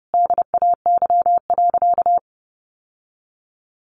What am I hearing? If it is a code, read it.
Morse code, heard as a keyed tone, about 30 wpm, 700 Hz. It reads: DAY.
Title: Sentences from Top 900 Words 30wpm